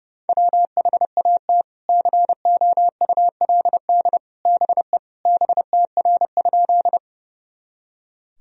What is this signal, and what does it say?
Morse code, heard as a keyed tone, about 30 wpm, 700 Hz. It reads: WHAT COULD BE BTR?
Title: Sentences from Top 400 Words and Abbreviations 30wpm